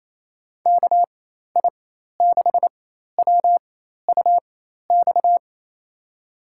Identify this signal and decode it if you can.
Morse code, heard as a keyed tone, about 28 wpm, spaced as for 13 wpm, 700 Hz. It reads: KI6WUX
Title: USA Call Signs - 6 Characters - 4x Character Spacing - 28wpm